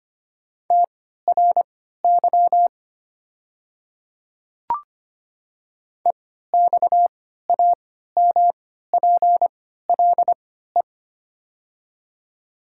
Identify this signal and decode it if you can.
Morse code, heard as a keyed tone, about 25 wpm, spaced as for 14 wpm, 700 Hz. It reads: TRY EXAMPLE
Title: 3x Character Spacing - Top 300 Words - 25wpm